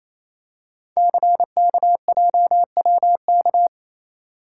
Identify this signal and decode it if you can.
Morse code, heard as a keyed tone, about 29 wpm, 700 Hz. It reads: CKJWK
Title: Code Groups - 5 Letters - 28wpm